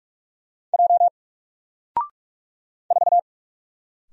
Morse code, heard as a keyed tone, about 44 wpm, 700 Hz. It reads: J 4
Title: Single Letter-Number 45wpm